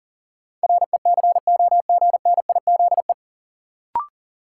Morse code, heard as a keyed tone, about 40 wpm, 700 Hz. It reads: RECOGNIZE
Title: Sentences from Top 1400 Words - New Word Drill - 40wpm